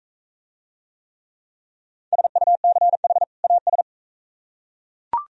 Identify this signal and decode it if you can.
Morse code, heard as a keyed tone, about 44 wpm, 700 Hz. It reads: SUCH AS